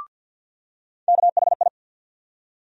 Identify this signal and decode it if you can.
Morse code, heard as a keyed tone, about 52 wpm, 700 Hz. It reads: KHI
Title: USA Call Sign Postfixes - 3 Letters 50wpm